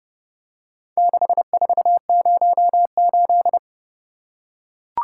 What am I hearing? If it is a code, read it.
Morse code, heard as a keyed tone, about 30 wpm, 700 Hz. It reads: 6408